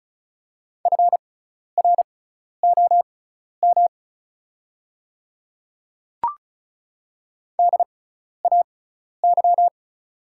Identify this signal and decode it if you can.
Morse code, heard as a keyed tone, about 36 wpm, spaced as for 12 wpm, 700 Hz. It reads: FROM DAY
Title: 6x Character Spacing - Top 100 Words - 35wpm